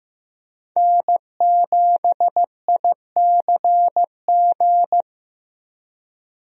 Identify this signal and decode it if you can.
Morse code, heard as a keyed tone, about 15 wpm, 700 Hz. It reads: N7ICG
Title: USA Call Signs - 5 Characters - Repeated 2x - 15wpm